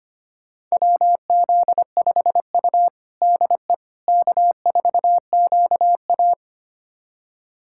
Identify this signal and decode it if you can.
Morse code, heard as a keyed tone, about 25 wpm, 700 Hz. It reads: WZ5U DE K4QA
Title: USA Callsign DE Callsign - 4 Characters - 25wpm